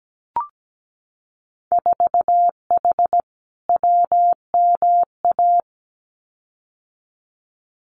Morse code, heard as a keyed tone, about 17 wpm, 700 Hz. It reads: 4H WMA
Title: Winter Field Day Exchanges Encore 17wpm